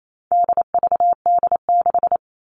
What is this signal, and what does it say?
Morse code, heard as a keyed tone, about 28 wpm, 700 Hz. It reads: DVD6
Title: All Cut Numbers - 4-Digit 28wpm